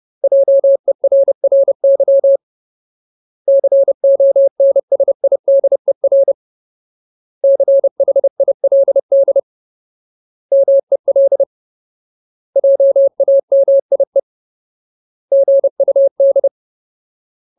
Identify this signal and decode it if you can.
Morse code, heard as a keyed tone, about 30 wpm, 550 Hz. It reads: JERRY CONSIDER CHILD MEL JAMIE GUD
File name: Ultimate Word List / 30WPM Ultimate